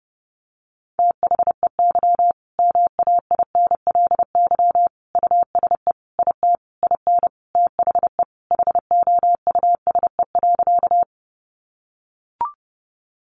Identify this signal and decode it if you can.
Morse code, heard as a keyed tone, about 30 wpm, 700 Hz. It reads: THEY MAINLY USE IT IN THE HOUSE.
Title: Sentences from Top 1400 Words 30wpm